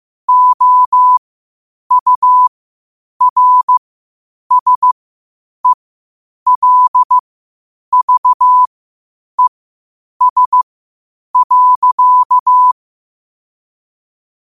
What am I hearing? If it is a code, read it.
Morse code, heard as a keyed tone, about 15 wpm, spaced as for 8 wpm, 1000 Hz. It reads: OURSELVES.
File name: qotdfiles / 8wpm